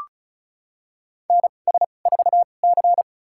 Morse code, heard as a keyed tone, about 36 wpm, spaced as for 25 wpm, 700 Hz. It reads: NS4C
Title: USA Call Signs - 4 Characters - 2x Character Spacing - 35wpm